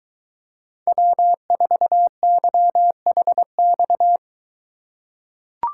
Morse code, heard as a keyed tone, about 22 wpm, 700 Hz. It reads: W4YHX